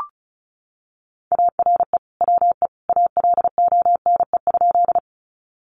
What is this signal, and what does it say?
Morse code, heard as a keyed tone, about 34 wpm, 700 Hz. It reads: ARE WE ALONE?
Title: Sentences from Top 1000 Words 35wpm